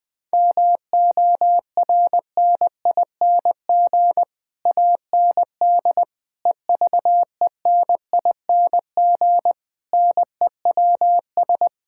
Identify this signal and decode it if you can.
Morse code, heard as a keyed tone, about 20 wpm, 700 Hz. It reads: MORNING AND EVENING NEWS
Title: Sets of 9 Words - 20wpm